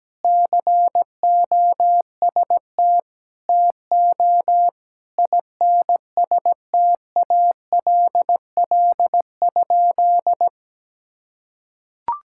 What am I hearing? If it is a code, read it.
Morse code, heard as a keyed tone, about 17 wpm, 700 Hz. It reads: COST TO INSTALL?
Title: Sentences from Top 1600 Words 17wpm